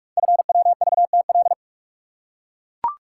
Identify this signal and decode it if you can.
Morse code, heard as a keyed tone, about 44 wpm, 700 Hz. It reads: FWVTL
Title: Code Groups - 5 Letter:Numbers - 45wpm